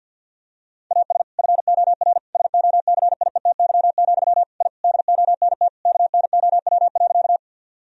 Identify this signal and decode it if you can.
Morse code, heard as a keyed tone, about 50 wpm, 700 Hz. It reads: AS FOR SOCIETY, I DONT KNOW.
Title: Sentences from Top 1000 Words - Review Part 4 - 50wpm